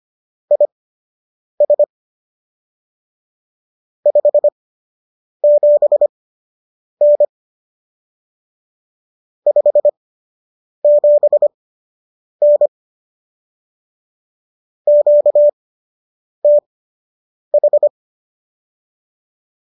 Morse code, heard as a keyed tone, about 25 wpm, spaced as for 8 wpm, 590 Hz. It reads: IS 57N 57N QTH